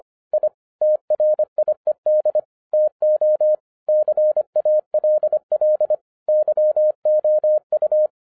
Morse code, heard as a keyed tone, about 25 wpm, 610 Hz. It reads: I TRIED TO CALL YOU
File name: 2011-07-mp3 / n34 THE 3.11 2 25wpm